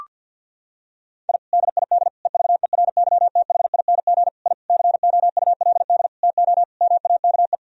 Morse code, heard as a keyed tone, about 53 wpm, 700 Hz. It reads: I DID EVERYTHING I COULD TO MAKE
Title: Sentences from Top 1000 Words - Review Part 1 - 50wpm